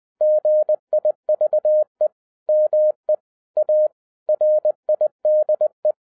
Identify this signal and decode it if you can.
Morse code, heard as a keyed tone, about 20 wpm, 610 Hz. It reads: GIVE ME A RIDE